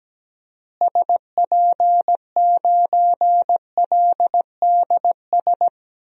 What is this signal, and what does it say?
Morse code, heard as a keyed tone, about 17 wpm, 700 Hz. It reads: SP9LDS
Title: DXCC-100 Call Signs - 6 Characters - Repeated 2x - 17wpm